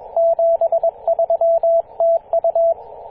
Morse code, heard as a keyed tone, about 22 wpm, 670 Hz, with radio noise. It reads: 73 TU
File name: posts / 2016-apr-24-1040z-v63gg-14-016mhz-cw